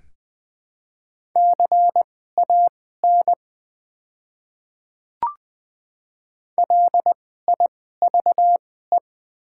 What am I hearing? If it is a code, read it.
Morse code, heard as a keyed tone, about 20 wpm, spaced as for 14 wpm, 700 Hz. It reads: CAN LIVE
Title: 2x Character Spacing - Top 200 Words - 20wpm